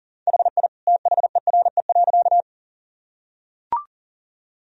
Morse code, heard as a keyed tone, about 37 wpm, 700 Hz. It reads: HI THERE.
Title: Sentences from Top 1000 Words - Review Part 3 - 40wpm